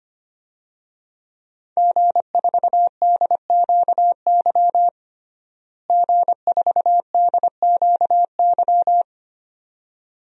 Morse code, heard as a keyed tone, about 25 wpm, 700 Hz. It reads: G4DQY G4DQY
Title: DXCC-100 Call Signs - 5 Characters - Repeated 2x - 25wpm